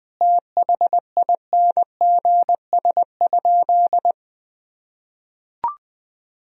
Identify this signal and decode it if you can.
Morse code, heard as a keyed tone, about 19 wpm, 700 Hz. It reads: THINGS?
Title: Sentences from Top 1000 Words - Review Part 4 - 20wpm